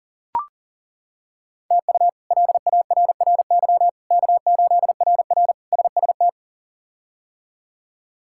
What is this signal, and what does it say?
Morse code, heard as a keyed tone, about 40 wpm, 700 Hz. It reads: TU LARRY K8RR SST
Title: K1USN SST QSO Practice - 40wpm